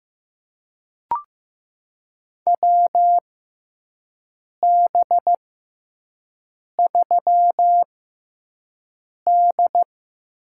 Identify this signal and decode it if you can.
Morse code, heard as a keyed tone, about 15 wpm, spaced as for 5 wpm, 700 Hz. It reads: WB3D
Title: USA Call Signs - 6 Characters - 6x Character Spacing - 15wpm